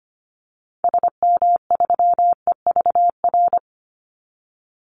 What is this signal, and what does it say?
Morse code, heard as a keyed tone, about 25 wpm, 700 Hz. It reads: SM3EVR
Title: Call Signs - 25wpm